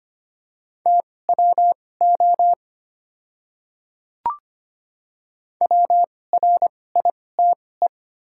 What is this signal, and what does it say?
Morse code, heard as a keyed tone, about 25 wpm, spaced as for 18 wpm, 700 Hz. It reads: TWO WRITE
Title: 2x Character Spacing - Top 200 Words - 25wpm